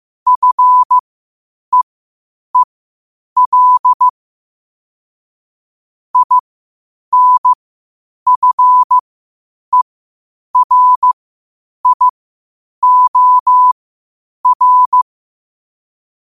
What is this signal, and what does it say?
Morse code, heard as a keyed tone, about 15 wpm, spaced as for 8 wpm, 1000 Hz. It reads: FEEL INFERIOR